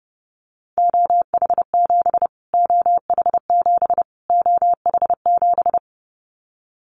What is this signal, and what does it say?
Morse code, heard as a keyed tone, about 28 wpm, 700 Hz. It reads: OH7 OH7 OH7
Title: Callsign Prefixes for Top 25 DXCC - Repeated 3x - 30wpm